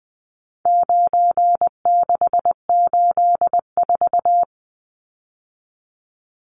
Morse code, heard as a keyed tone, about 19 wpm, 700 Hz. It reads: 9684
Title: Numbers - 4-Digit 20wpm